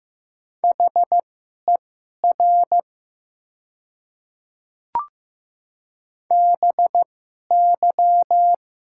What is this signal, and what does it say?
Morse code, heard as a keyed tone, about 15 wpm, spaced as for 11 wpm, 700 Hz. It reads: HER BY
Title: 2x Character Spacing - Top 200 Words - 15wpm